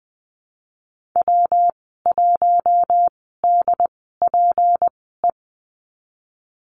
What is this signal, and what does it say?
Morse code, heard as a keyed tone, about 19 wpm, spaced as for 14 wpm, 700 Hz. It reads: W1DPE
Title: 238 Lesson 38g - Letter q - Call Signs 20wpm